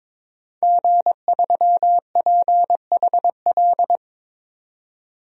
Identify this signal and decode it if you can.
Morse code, heard as a keyed tone, about 22 wpm, 700 Hz. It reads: G3PHL
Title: DXCC-100 Call Signs - 5 Characters - 22wpm